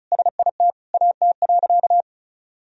Morse code, heard as a keyed tone, about 36 wpm, 700 Hz. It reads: SIT AT.
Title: Sentences from Top 1000 Words - Review Part 2 - 35wpm